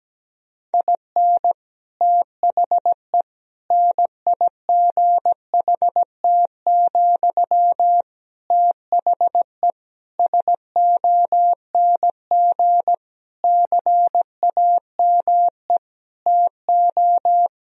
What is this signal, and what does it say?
Morse code, heard as a keyed tone, about 17 wpm, 700 Hz. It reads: IN THE NIGHT, THE SONG CAME TO
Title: Sentences from Top 1000 Words - Review Part 2 - 17wpm